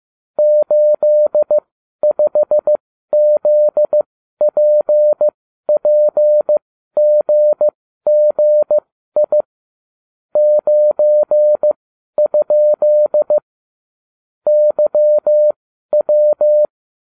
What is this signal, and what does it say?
Morse code, heard as a keyed tone, about 15 wpm, spaced as for 12 wpm, 600 Hz. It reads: 85ZPPGGI 9? YW